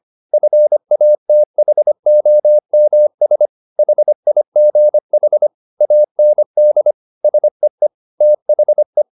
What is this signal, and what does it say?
Morse code, heard as a keyed tone, about 25 wpm, 600 Hz. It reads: FATHOMS HIGH AND SEE THE